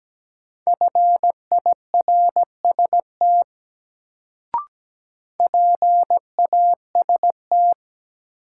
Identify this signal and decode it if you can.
Morse code, heard as a keyed tone, about 17 wpm, 700 Hz. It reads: FIRST PAST
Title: Sets of 2 Words - New Word Drill - 17wpm